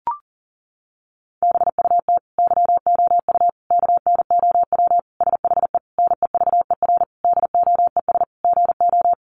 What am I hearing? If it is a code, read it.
Morse code, heard as a keyed tone, about 40 wpm, 700 Hz. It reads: BUT YOU KNOW SHE NEVER DOES GO
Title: Sentences from Top 1400 Words 40wpm